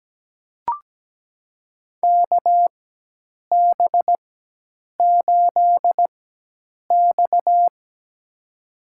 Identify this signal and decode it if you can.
Morse code, heard as a keyed tone, about 17 wpm, spaced as for 8 wpm, 700 Hz. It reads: KB8X
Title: USA Call Signs - 4x Character Spacing - 17wpm